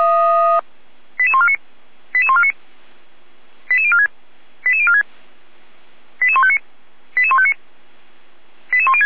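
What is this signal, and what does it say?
Five-tone selective call (ZVEI), heard as quick runs of five tones, tones of about 70 ms. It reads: 88149 88149 88046 88046 88159 88159 88106
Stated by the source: ZVEI five-tone selective call 88102, received on an SDR, from the Feuerwehr Orlamünde server.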